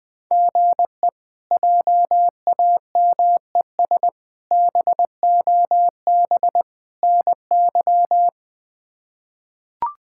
Morse code, heard as a keyed tone, about 21 wpm, 700 Hz. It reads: GE JAMES BOB NY
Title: K1USN SST QSO Practice - 20wpm